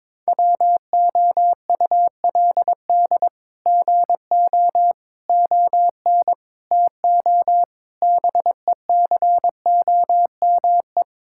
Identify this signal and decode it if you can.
Morse code, heard as a keyed tone, about 22 wpm, 700 Hz. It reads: WOULD GO ON TO BECOME
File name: practice / 22wpm / Sets of 8 Words - 22wpm